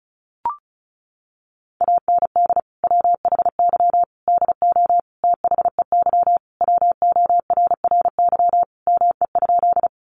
Morse code, heard as a keyed tone, about 34 wpm, 700 Hz. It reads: AND WHY DO THEY WORRY ME?
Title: Sentences from Top 1100 Words 35wpm